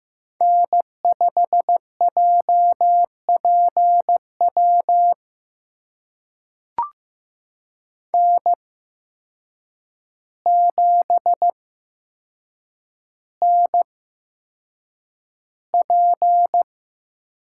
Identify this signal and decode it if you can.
Morse code, heard as a keyed tone, about 15 wpm, spaced as for 7 wpm, 700 Hz. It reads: N5JPW N7NP
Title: USA Call Signs - 8x Character Spacing - 15wpm